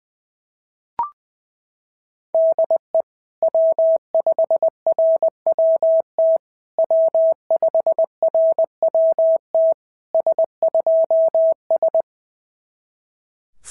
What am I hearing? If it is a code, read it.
Morse code, heard as a keyed tone, about 18 wpm, 650 Hz. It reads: DE W5RWT W5RWT S2S
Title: SOTA QSO Practice 20wpm